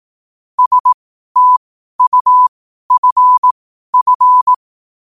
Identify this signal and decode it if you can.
Morse code, heard as a keyed tone, about 18 wpm, spaced as for 12 wpm, 1000 Hz. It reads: STUFF
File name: qotdfiles / 12wpm